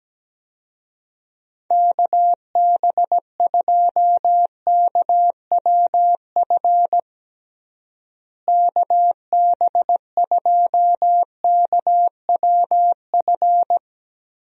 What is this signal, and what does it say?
Morse code, heard as a keyed tone, about 17 wpm, 700 Hz. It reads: KB2KWF KB2KWF
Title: USA Call Signs - 6 Characters - Repeated 2x - 17wpm